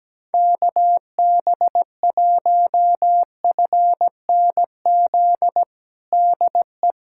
Morse code, heard as a keyed tone, about 17 wpm, 700 Hz. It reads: KB1FNZ DE
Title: USA Callsign DE Callsign - 6 Characters - 17wpm